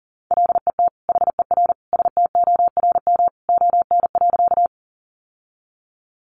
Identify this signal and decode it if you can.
Morse code, heard as a keyed tone, about 40 wpm, 700 Hz. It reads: LET HER STORM ON.
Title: Sentences from Top 1500 Words 40wpm